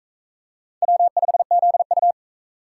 Morse code, heard as a keyed tone, about 40 wpm, 700 Hz. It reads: W5ZU